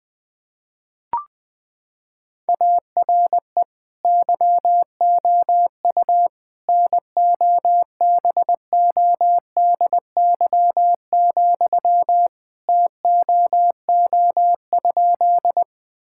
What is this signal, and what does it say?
Morse code, heard as a keyed tone, about 21 wpm, 700 Hz. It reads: ARE YOU NOBODY, TOO?
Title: Sentences from Top 1600 Words 20wpm